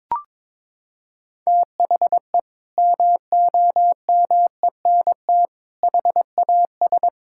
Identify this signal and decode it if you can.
Morse code, heard as a keyed tone, about 22 wpm, 700 Hz. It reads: THE MOMENT HAS
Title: Sentences from Top 1000 Words - Review Part 5 - 22wpm